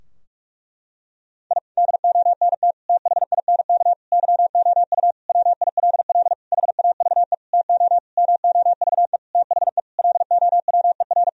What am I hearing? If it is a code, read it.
Morse code, heard as a keyed tone, about 46 wpm, 700 Hz. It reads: I DONT THINK YOU WILL HAVE TO MOVE THE LOWER
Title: Sentences from Top 1000 Words - Review Part 4 - 45wpm